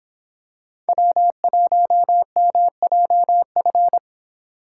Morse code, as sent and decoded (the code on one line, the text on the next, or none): .-- .---- -- .--- ..-.
W1MJF